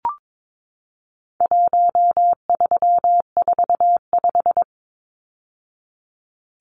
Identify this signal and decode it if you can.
Morse code, heard as a keyed tone, about 22 wpm, 700 Hz. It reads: 1345
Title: Numbers - 4-Digit 22wpm